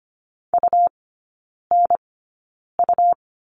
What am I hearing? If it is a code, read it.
Morse code, heard as a keyed tone, about 25 wpm, spaced as for 9 wpm, 700 Hz. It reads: UNU
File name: Lesson 17e - Letter u - Three Characters 25wpm